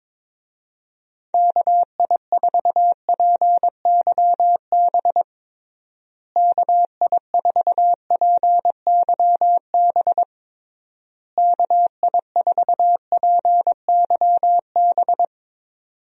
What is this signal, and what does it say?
Morse code, heard as a keyed tone, about 22 wpm, 700 Hz. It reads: KI4PYB KI4PYB KI4PYB
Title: USA Call Signs - 6 Characters - Repeated 3x - 22wpm